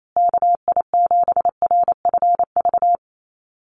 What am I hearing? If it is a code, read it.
Morse code, heard as a keyed tone, about 28 wpm, 700 Hz. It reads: KI7RFV
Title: USA Call Signs Repeated 2x 28wpm